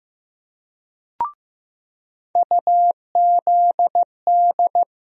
Morse code, heard as a keyed tone, about 15 wpm, 700 Hz. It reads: UZD
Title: USA Call Sign Postfixes - 3 Letters 15wpm